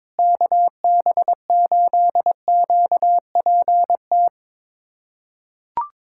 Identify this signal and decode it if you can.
Morse code, heard as a keyed tone, about 22 wpm, 700 Hz. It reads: KB8QPT
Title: USA Callsign DE Callsign - 6 Characters - 22wpm